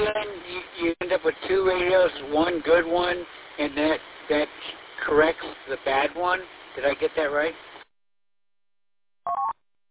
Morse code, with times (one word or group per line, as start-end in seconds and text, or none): none